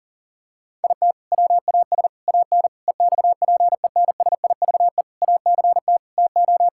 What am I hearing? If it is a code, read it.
Morse code, heard as a keyed tone, about 37 wpm, 700 Hz. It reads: IT WAS AN EXPENSIVE ACT TO
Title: Sentences from Top 1000 Words - Review Part 6 - 40wpm